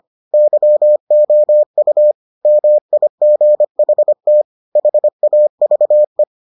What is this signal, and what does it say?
Morse code, heard as a keyed tone, about 25 wpm, 600 Hz. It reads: YOU MIGHT HAVE